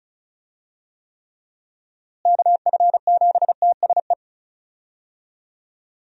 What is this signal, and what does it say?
Morse code, heard as a keyed tone, about 34 wpm, 700 Hz. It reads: KF7TSE
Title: USA Call Signs - 6 Characters - Repeated 3x - 35wpm